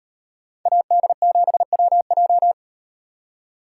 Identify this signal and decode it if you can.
Morse code, heard as a keyed tone, about 38 wpm, 700 Hz. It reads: AD7WJ